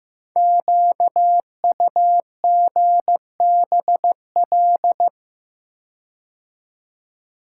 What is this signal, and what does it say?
Morse code, heard as a keyed tone, about 15 wpm, 700 Hz. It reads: QUGBL
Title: Code Groups - 5 Letters - 15wpm